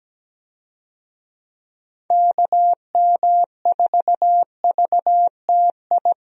Morse code, heard as a keyed tone, about 17 wpm, 700 Hz. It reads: KM4VTI
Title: USA Call Signs - 6 Characters - Repeated 2x - 17wpm